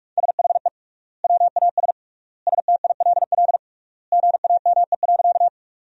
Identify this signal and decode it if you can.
Morse code, heard as a keyed tone, about 42 wpm, 700 Hz. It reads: SHE WAS STILL GAME.
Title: 3x Word-Spacing - Sentences from Top 500 Words - 45wpm